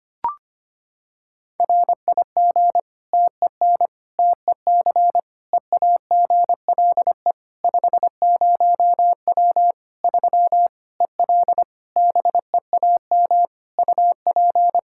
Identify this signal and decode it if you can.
Morse code, heard as a keyed tone, about 25 wpm, 700 Hz. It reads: RIG TEN TEC EAGLE 50W 3 EL BEAM UP
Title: Practice QSO - Exchange 2 - Info - 25wpm